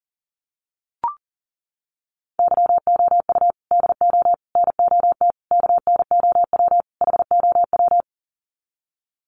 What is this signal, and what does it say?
Morse code, heard as a keyed tone, about 40 wpm, 700 Hz. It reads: YOU DO NOT KNOW HOW
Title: Sets of 5 Words Encore 40wpm